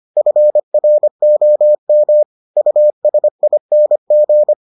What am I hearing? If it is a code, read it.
Morse code, heard as a keyed tone, about 25 wpm, 600 Hz. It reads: FROM USING